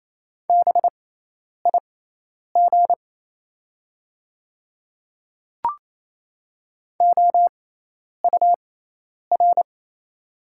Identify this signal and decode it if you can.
Morse code, heard as a keyed tone, about 27 wpm, spaced as for 10 wpm, 700 Hz. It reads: BIG OUR
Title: 6x Character Spacing - Top 200 Words - 28wpm